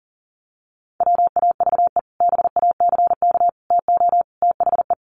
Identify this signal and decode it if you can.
Morse code, heard as a keyed tone, about 40 wpm, 700 Hz. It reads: WAVE BACK TO THE
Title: Sentences from Top 1400 Words 40wpm